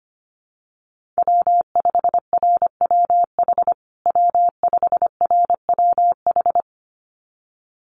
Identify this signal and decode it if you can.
Morse code, heard as a keyed tone, about 25 wpm, 700 Hz. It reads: W5RWH W5RWH
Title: USA Call Signs Repeated 2x 25wpm